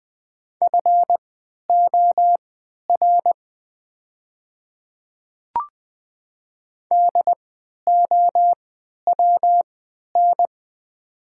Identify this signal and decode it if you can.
Morse code, heard as a keyed tone, about 20 wpm, spaced as for 11 wpm, 700 Hz. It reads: FOR DOWN
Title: 3x Character Spacing - Top 500 Words - 20wpm